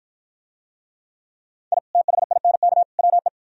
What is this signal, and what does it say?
Morse code, heard as a keyed tone, about 50 wpm, 700 Hz. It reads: I THINK WE